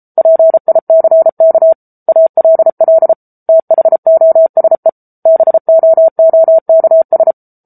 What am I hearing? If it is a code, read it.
Morse code, heard as a keyed tone, about 34 wpm, 650 Hz. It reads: PICK ALL THOSE BOOKS
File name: Book Two / 07-Chapter 7 - 33 WPM